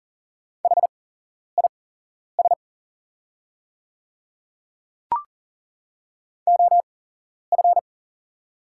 Morse code, heard as a keyed tone, about 40 wpm, spaced as for 11 wpm, 700 Hz. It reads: HIS OF
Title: 8x Character Spacing - Top 200 Words - 40wpm